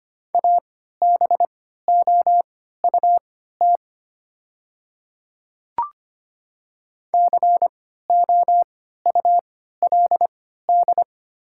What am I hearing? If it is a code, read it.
Morse code, heard as a keyed tone, about 25 wpm, spaced as for 14 wpm, 690 Hz. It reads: ABOUT COULD
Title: 3x Character Spacing - Top 100 Words - 25wpm